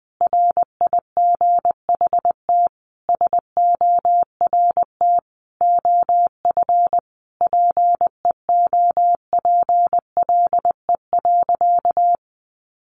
Morse code, heard as a keyed tone, about 20 wpm, 700 Hz. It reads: RIGHT SORT OF PEOPLE.
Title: Sentences from Top 1300 Words 20wpm